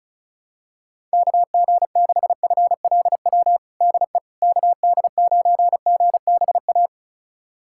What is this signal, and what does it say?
Morse code, heard as a keyed tone, about 34 wpm, 690 Hz. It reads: KG6FLW DE KD9GBA
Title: USA Callsign DE Callsign 35wpm